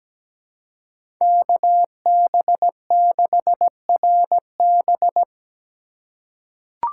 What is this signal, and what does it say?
Morse code, heard as a keyed tone, about 17 wpm, 700 Hz. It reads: KB6RB
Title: USA Call Signs - 5 Characters - 4x Character Spacing - 17wpm